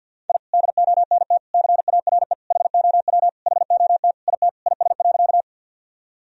Morse code, heard as a keyed tone, about 47 wpm, 700 Hz. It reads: I DONT CARE HOW HOT IT IS.
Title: Sentences from Top 1000 Words - Review Part 3 - 50wpm